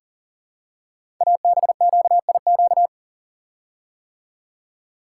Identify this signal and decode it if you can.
Morse code, heard as a keyed tone, about 40 wpm, 700 Hz. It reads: ABQIQ
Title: Code Groups - 5 Letters - 40wpm